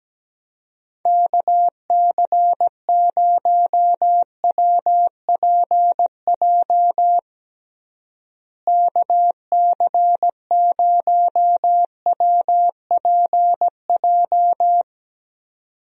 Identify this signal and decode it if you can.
Morse code, heard as a keyed tone, about 17 wpm, 700 Hz. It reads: KC0WPJ KC0WPJ
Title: USA Call Signs - 6 Characters - Repeated 2x - 17wpm